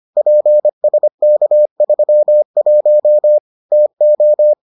Morse code, heard as a keyed tone, about 25 wpm, 600 Hz. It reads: PSK31 TO